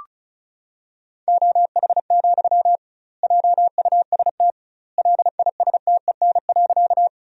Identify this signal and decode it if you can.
Morse code, heard as a keyed tone, about 34 wpm, 700 Hz. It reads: OH, JUST LISTEN.
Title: 2x Word-Spacing - Sentences from Top 500 Words - 35wpm